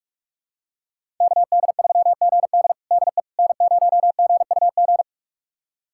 Morse code, heard as a keyed tone, about 46 wpm, 700 Hz. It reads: KD3GD DE N0GUG
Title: USA Callsign DE Callsign - 5 Characters - 45wpm